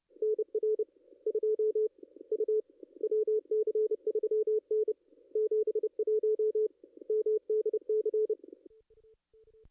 Morse code, heard as a keyed tone, about 30 wpm, 440 Hz. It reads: NR 2 U WC3N 71 MDC
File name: contests / ss-cw-k5tr-op-n5rz -2200